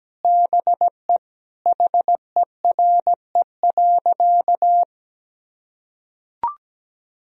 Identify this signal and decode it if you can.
Morse code, heard as a keyed tone, about 17 wpm, 700 Hz. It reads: BE HERE.
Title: Sentences from Top 1000 Words - Review Part 1 - 17wpm